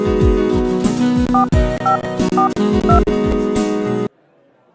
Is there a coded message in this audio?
Yes: DTMF from 1.3 to 3.0 s, heard as telephone touch tones, tones of about 110 ms. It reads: 4542